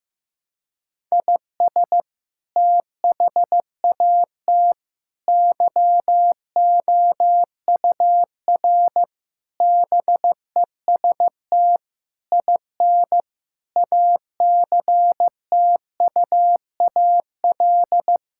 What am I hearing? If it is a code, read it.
Morse code, heard as a keyed tone, about 15 wpm, 700 Hz. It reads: IS THAT YOUR BEST IN ACTUAL
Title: Sentences from Top 1000 Words - Review Part 5 - 15wpm